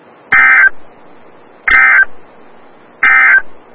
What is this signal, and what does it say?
Emergency Alert System (SAME) tones, heard as harsh buzzing bursts. It is heard from 0.2 to 3.5 s.
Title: Des Moines, IA NOAA Weather Radio - WXL57